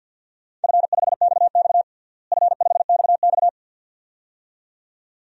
Morse code, heard as a keyed tone, about 48 wpm, 700 Hz. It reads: F5XX F5XX
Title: DXCC-100 Call Signs - 4 Characters - Repeated 2x - 50wpm